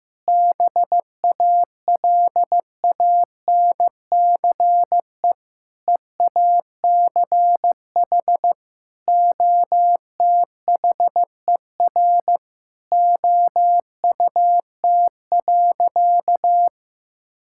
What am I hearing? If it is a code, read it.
Morse code, heard as a keyed tone, about 15 wpm, 700 Hz. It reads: BALANCE EACH OTHER OUT.
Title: Sentences from Top 1000 Words - Review Part 6 - 15wpm